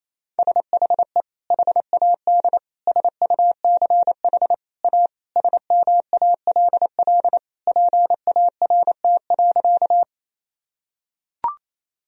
Morse code, heard as a keyed tone, about 27 wpm, 700 Hz. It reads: SHE HAD SUCH A SMALL PART.
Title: Sentences from Top 1000 Words - Review Part 1 - 28wpm